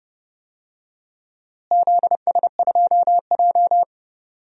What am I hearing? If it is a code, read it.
Morse code, heard as a keyed tone, about 30 wpm, 700 Hz. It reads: ZS2J
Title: DXCC-100 Call Signs - 4 Characters - Repeated 3x - 30wpm